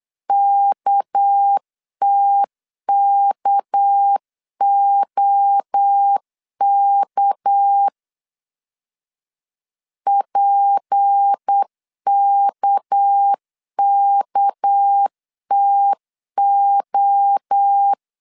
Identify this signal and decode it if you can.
Morse code, heard as a keyed tone, about 8 wpm, 800 Hz. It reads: KTKOK PKKTO
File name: LEZIONI / 6 PKX0 / 6 PK 8wpm 3003